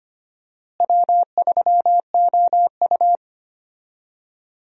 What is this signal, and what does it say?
Morse code, heard as a keyed tone, about 25 wpm, 700 Hz. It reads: W3OU
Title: USA Call Signs - 4 Characters - Repeated 2x - 25wpm